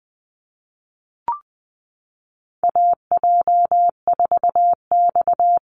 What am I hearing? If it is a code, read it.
Morse code, heard as a keyed tone, about 20 wpm, 700 Hz. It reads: AJ4X
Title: USA Call Signs Repeated 2x 20wpm